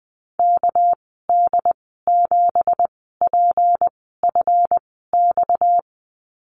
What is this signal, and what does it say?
Morse code, heard as a keyed tone, about 20 wpm, spaced as for 14 wpm, 700 Hz. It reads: KD7PFX
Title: 245 Lesson 39g - Letter z - Call Signs 20wpm